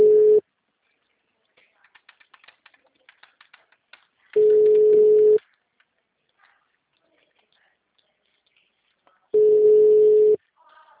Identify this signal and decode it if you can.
Ringback tone (Europe, 425 Hz).